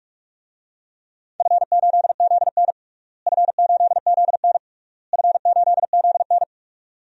Morse code, heard as a keyed tone, about 46 wpm, 700 Hz. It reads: F8ZN F8ZN F8ZN
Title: DXCC-100 Call Signs - 4 Characters - Repeated 3x - 45wpm